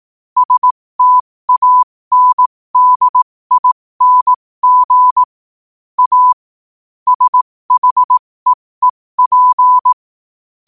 Morse code, heard as a keyed tone, about 18 wpm, spaced as for 15 wpm, 1000 Hz. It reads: STANDING A SHEEP
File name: qotdfiles / 15wpm